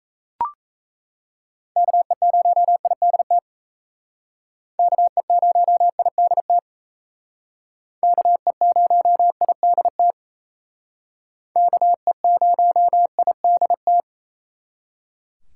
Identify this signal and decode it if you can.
Morse code, heard as a keyed tone, about 36 wpm, 700 Hz. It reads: KE0IDT KE0IDT KE0IDT KE0IDT